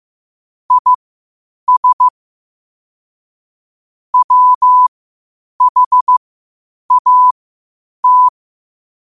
Morse code, heard as a keyed tone, about 15 wpm, spaced as for 8 wpm, 1000 Hz. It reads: IS WHAT
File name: qotdfiles / 8wpm